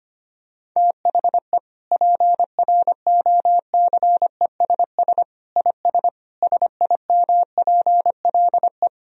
Morse code, heard as a keyed tone, about 25 wpm, 700 Hz. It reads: THE PROCESS IS SIMPLE
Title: Sentences from Top 1000 Words - Review Part 3 - 25wpm